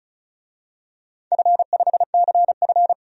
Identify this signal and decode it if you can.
Morse code, heard as a keyed tone, about 35 wpm, 700 Hz. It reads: F5CF
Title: DXCC-100 Call Signs - 4 Characters - 35wpm